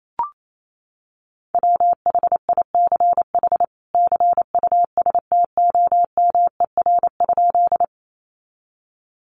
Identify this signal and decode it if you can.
Morse code, heard as a keyed tone, about 28 wpm, 700 Hz. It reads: WHICH CUSTOMER?
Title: Sentences from Top 1100 Words 28wpm